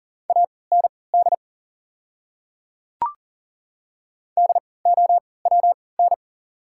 Morse code, heard as a keyed tone, about 40 wpm, spaced as for 23 wpm, 690 Hz. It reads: AND DOWN
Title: 3x Character Spacing - Top 200 Words - 40wpm